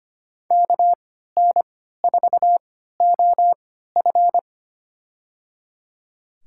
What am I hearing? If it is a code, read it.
Morse code, heard as a keyed tone, about 25 wpm, spaced as for 14 wpm, 700 Hz. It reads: KN4OF
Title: USA Call Signs - 5 Characters - 3x Character Spacing - 25wpm